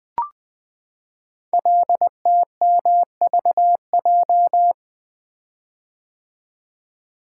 Morse code, heard as a keyed tone, about 20 wpm, 700 Hz. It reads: LTMVJ
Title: Code Groups - 5 Letters - 20wpm